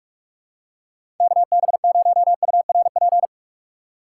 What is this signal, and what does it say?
Morse code, heard as a keyed tone, about 46 wpm, 700 Hz. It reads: KB0URP
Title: USA Call Signs - 6 Characters - 4x Character Spacing - 45wpm